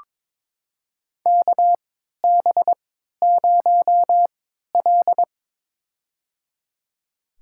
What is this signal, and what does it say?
Morse code, heard as a keyed tone, about 22 wpm, spaced as for 13 wpm, 700 Hz. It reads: KB0L
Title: USA Call Signs - 4 Characters - 3x Character Spacing - 22wpm